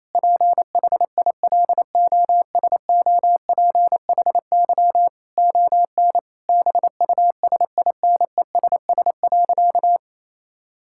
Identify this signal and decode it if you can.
Morse code, heard as a keyed tone, about 28 wpm, 700 Hz. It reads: PHILOSOPHY ON BUSINESS.
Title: Sentences from Top 1000 Words - Review Part 6 - 28wpm